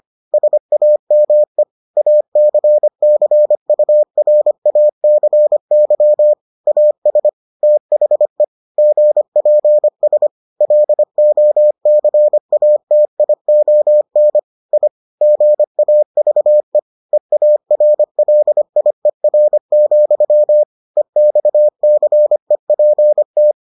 Morse code, heard as a keyed tone, about 25 wpm, 600 Hz. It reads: SAME ACCURACY AS THE GPS LOCATION I GAVE EARLIER, EXCEPT